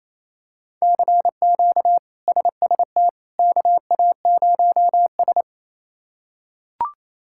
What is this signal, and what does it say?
Morse code, heard as a keyed tone, about 27 wpm, 700 Hz. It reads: CQ SST KA0S
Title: K1USN SST QSO Practice - 28wpm